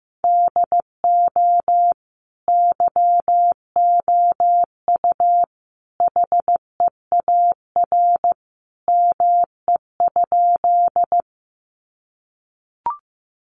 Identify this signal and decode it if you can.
Morse code, heard as a keyed tone, about 15 wpm, 700 Hz. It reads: DO YOU HEAR ME?
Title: Sentences from Top 300 Words 15wpm